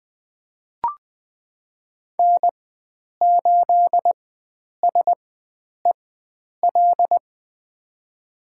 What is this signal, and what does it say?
Morse code, heard as a keyed tone, about 20 wpm, spaced as for 9 wpm, 700 Hz. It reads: N8SEL
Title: USA Call Signs - 5 Characters - 4x Character Spacing - 20wpm